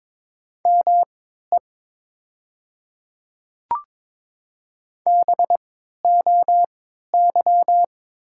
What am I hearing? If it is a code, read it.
Morse code, heard as a keyed tone, about 22 wpm, spaced as for 12 wpm, 690 Hz. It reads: ME BOY
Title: 3x Character Spacing - Top 200 Words - 22wpm